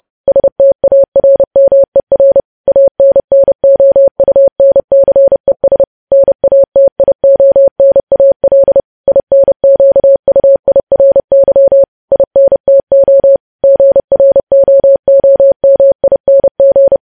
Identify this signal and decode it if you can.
Morse code, heard as a keyed tone, about 30 wpm, 550 Hz. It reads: STARMER ANNOUNCES NATIONAL INQUIRY INTO GROOMING
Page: News Headlines in Morse Code at 30 WPM